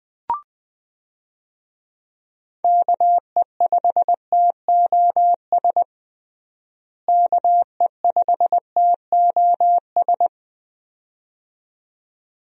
Morse code, heard as a keyed tone, about 20 wpm, 700 Hz. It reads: KE5TOS KE5TOS
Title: USA Call Signs - 6 Characters - Repeated 2x - 20wpm